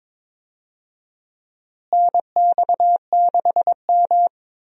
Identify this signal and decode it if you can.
Morse code, heard as a keyed tone, about 22 wpm, 700 Hz. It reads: NX6M
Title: USA Call Signs - 4 Characters - Repeated 2x - 22wpm